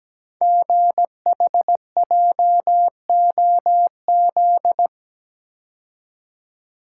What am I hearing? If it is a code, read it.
Morse code, heard as a keyed tone, about 17 wpm, 700 Hz. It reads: GHJOZ
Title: Code Groups - 5 Letters - 17wpm